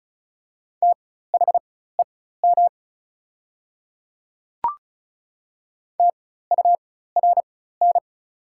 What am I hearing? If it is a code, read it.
Morse code, heard as a keyed tone, about 36 wpm, spaced as for 16 wpm, 700 Hz. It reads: THEM TURN
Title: 4x Character Spacing - Top 300 Words - 35wpm